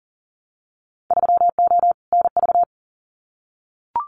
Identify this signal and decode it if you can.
Morse code, heard as a keyed tone, about 40 wpm, 700 Hz. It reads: 3O NV
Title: Winter Field Day Exchanges Encore 40wpm